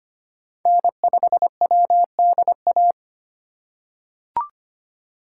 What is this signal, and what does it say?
Morse code, heard as a keyed tone, about 25 wpm, 700 Hz. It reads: N5WDA
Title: USA Call Signs - 5 Characters - 3x Character Spacing - 25wpm